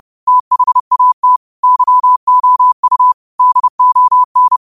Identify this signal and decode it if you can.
Morse code, heard as a keyed tone, about 30 wpm, 1000 Hz. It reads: THAT YOU DON
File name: 30wpm